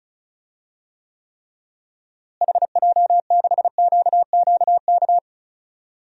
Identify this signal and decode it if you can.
Morse code, heard as a keyed tone, about 36 wpm, 700 Hz. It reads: HJ6QQK
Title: DXCC-100 Call Signs - 6 Characters - Repeated 3x - 35wpm